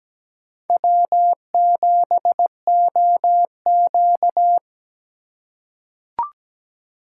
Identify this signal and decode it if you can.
Morse code, heard as a keyed tone, about 17 wpm, 700 Hz. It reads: W7OQ
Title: K1USN SST QSO Practice - 17wpm